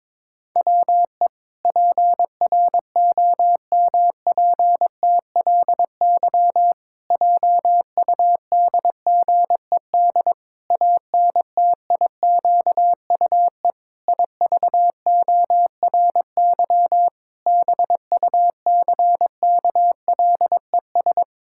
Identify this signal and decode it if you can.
Morse code, heard as a keyed tone, about 22 wpm, 700 Hz. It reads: WE PROMPTLY JUDGED ANTIQUE IVORY BUCKLES
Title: Pangram Challenge v2 - 22wpm